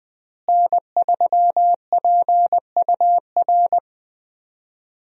Morse code, heard as a keyed tone, about 20 wpm, 700 Hz. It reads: N3PUR
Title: USA Call Signs - 5 Characters - Repeated 3x - 20wpm